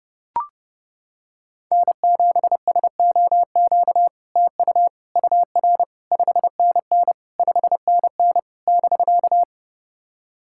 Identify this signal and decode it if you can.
Morse code, heard as a keyed tone, about 31 wpm, 700 Hz. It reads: N7SOQ TU UR 5NN 5NN <BK>
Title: SOTA QSO Practice 30wpm